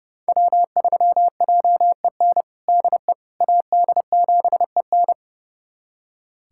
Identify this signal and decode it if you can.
Morse code, heard as a keyed tone, about 30 wpm, 700 Hz. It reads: W3JEN DE AD7EN
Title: USA Callsign DE Callsign - 5 Characters - 30wpm